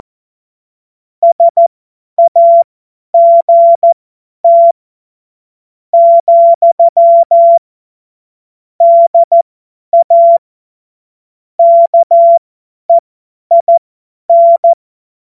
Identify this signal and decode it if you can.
Morse code, heard as a keyed tone, about 14 wpm, spaced as for 9 wpm, 680 Hz. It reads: SAGT, DA KEIN
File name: Morsekurs / Witze / Witz 14. Juni 2022 50 F70